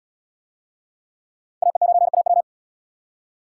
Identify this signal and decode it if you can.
Morse code, heard as a keyed tone, about 74 wpm, 700 Hz. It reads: AE9AW